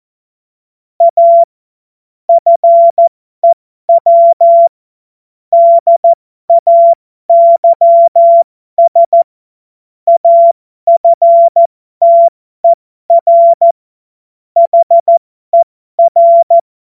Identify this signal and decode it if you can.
Morse code, heard as a keyed tone, about 14 wpm, spaced as for 12 wpm, 680 Hz. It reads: A FEW DAYS AFTER HER
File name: Morsekurs / Witze